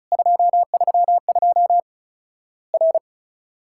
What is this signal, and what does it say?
Morse code, heard as a keyed tone, about 36 wpm, 700 Hz. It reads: 232 R
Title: Worked All Europe DX Contest - QTC Exchange - 35wpm